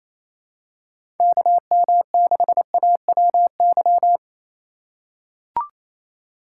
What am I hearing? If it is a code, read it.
Morse code, heard as a keyed tone, about 28 wpm, 700 Hz. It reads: KM6AWY
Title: USA Call Signs - 6x Character Spacing - 28wpm